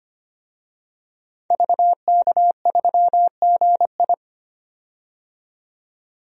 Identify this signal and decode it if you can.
Morse code, heard as a keyed tone, about 25 wpm, 700 Hz. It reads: VK3GI
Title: DXCC-100 Call Signs - 5 Characters - 25wpm